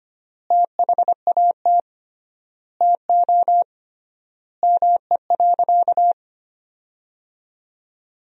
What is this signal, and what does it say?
Morse code, heard as a keyed tone, about 25 wpm, 700 Hz. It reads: THAT TO ME.
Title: 3x Word-Spacing - Sentences from Top 300 Words - 25wpm